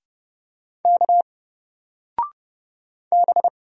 Morse code, heard as a keyed tone, about 30 wpm, 700 Hz. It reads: K B
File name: Single-Letter-Number-30wpm